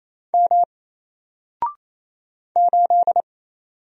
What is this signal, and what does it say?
Morse code, heard as a keyed tone, about 28 wpm, 700 Hz. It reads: M 8
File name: Single-Letter-Number-28wpm